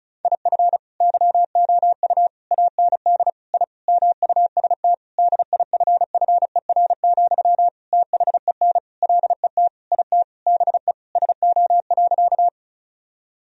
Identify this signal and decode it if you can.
Morse code, heard as a keyed tone, about 36 wpm, 700 Hz. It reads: IF YOU AND I MUST DIFFER, THEN LET IT BE SO.
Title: Sentences from Top 1000 Words - Review Part 1 - 35wpm